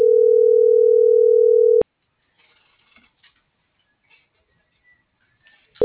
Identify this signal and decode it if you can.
Ringback tone (North America), 440 + 480 Hz.